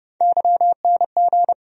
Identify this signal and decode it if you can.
Morse code, heard as a keyed tone, about 30 wpm, 700 Hz. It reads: YNG
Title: USA Call Sign Postfixes - 3 Letters 30wpm